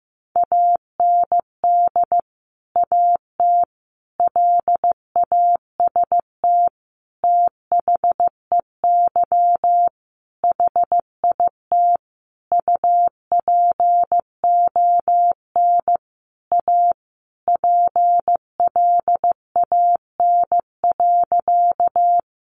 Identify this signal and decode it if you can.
Morse code, heard as a keyed tone, about 14 wpm, 700 Hz. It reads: AND AT LAST THEY HIT UPON A PLAN.
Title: Sentences from Top 1200 Words 15wpm